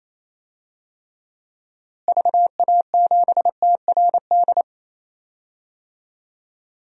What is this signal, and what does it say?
Morse code, heard as a keyed tone, about 28 wpm, 700 Hz. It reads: VA7TRD
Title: DXCC-100 Call Signs - 6 Characters - Repeated 2x - 28wpm